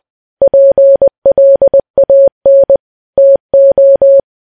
Morse code, heard as a keyed tone, about 20 wpm, 550 Hz. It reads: PLAN TO
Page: news-headlines-in-morse-code-at-20-wpm